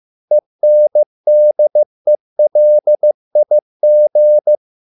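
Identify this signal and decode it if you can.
Morse code, heard as a keyed tone, about 15 wpm, 600 Hz. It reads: ENDELIG
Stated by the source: Morse code at a slow 15 wpm from CW med Gnister.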